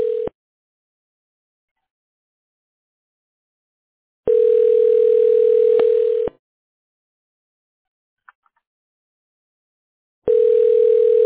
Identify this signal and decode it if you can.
Ringback tone (North America), 440 + 480 Hz.